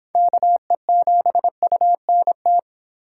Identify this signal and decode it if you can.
Morse code, heard as a keyed tone, about 26 wpm, 700 Hz. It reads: KE7UNT